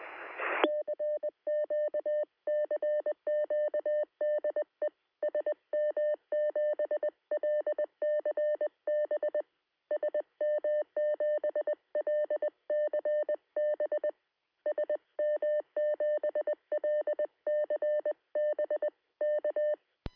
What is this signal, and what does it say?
Morse code, heard as a keyed tone, about 19 wpm, 600 Hz. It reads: CQ CQ DE SM7LCB SM7LCB SM7LCB K
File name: sm7lcb / 2122z g3xdy 13cm